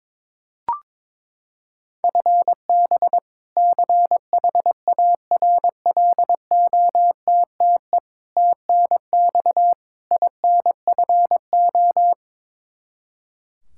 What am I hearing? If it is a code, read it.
Morse code, heard as a keyed tone, about 22 wpm, 700 Hz. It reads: FB CHARLOTTE TNX INFO
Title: Practice QSO - Exchange 3 - More Info - 22wpm